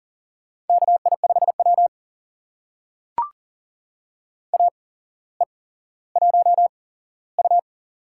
Morse code, heard as a keyed tone, about 40 wpm, spaced as for 17 wpm, 700 Hz. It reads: KI5W AE1U
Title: USA Call Signs - 4 Characters - 8x Character Spacing - 40wpm